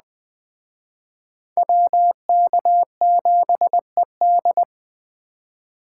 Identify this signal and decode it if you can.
Morse code, heard as a keyed tone, about 20 wpm, 700 Hz. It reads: WK7ED